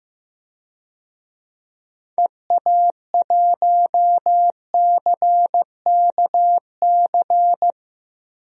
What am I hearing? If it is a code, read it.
Morse code, heard as a keyed tone, about 15 wpm, 700 Hz. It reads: EA1CKC
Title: DXCC-100 Call Signs - 6 Characters - 15wpm